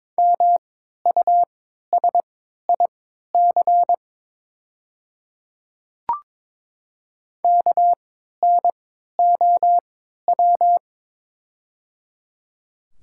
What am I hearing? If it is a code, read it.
Morse code, heard as a keyed tone, about 22 wpm, spaced as for 13 wpm, 700 Hz. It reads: MUSIC KNOW
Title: 3x Character Spacing - Top 300 Words - 22wpm